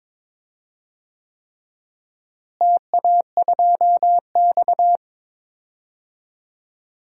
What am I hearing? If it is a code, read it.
Morse code, heard as a keyed tone, about 22 wpm, 700 Hz. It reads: TA2X
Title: DXCC-100 Call Signs - 4 Characters - Repeated 3x - 22wpm